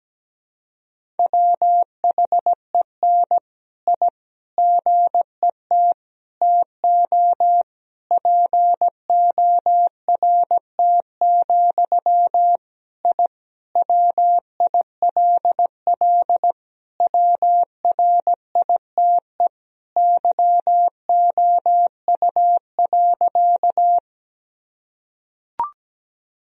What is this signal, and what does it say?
Morse code, heard as a keyed tone, about 17 wpm, 700 Hz. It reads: WHEN I GET TO PORT, I WILL WRITE YOU.
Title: Sentences from Top 1000 Words - Review Part 1 - 17wpm